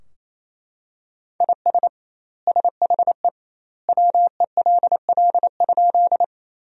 Morse code, heard as a keyed tone, about 27 wpm, 700 Hz. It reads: IS SHE WELL?
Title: 2x Word-Spacing - Sentences from Top 200 Words - 28wpm